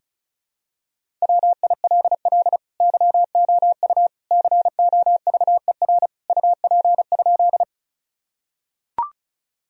Morse code, heard as a keyed tone, about 36 wpm, 700 Hz. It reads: WILL YOU COVER UP?
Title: Sentences from Top 1000 Words - Review Part 1 - 35wpm